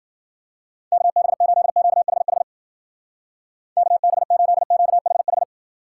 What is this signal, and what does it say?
Morse code, heard as a keyed tone, about 48 wpm, 700 Hz. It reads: KB8QHH KB8QHH
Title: Speed Racing - USA 2x3 Call Signs 40wpm